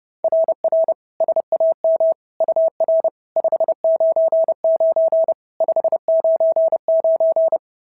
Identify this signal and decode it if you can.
Morse code, heard as a keyed tone, about 30 wpm, 690 Hz. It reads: RR SAM UR 599 599
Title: SOTA QSO Practice 30wpm